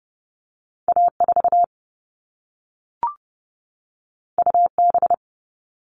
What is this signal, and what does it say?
Morse code, heard as a keyed tone, about 30 wpm, 700 Hz. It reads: A4 UB
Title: All Cut Numbers - 2-Digit 30wpm